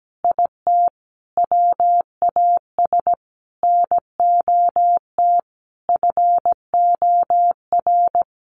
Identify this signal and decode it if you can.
Morse code, heard as a keyed tone, about 17 wpm, 700 Hz. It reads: IT WAS NOT FOR
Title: Sets of 5 Words Encore 17wpm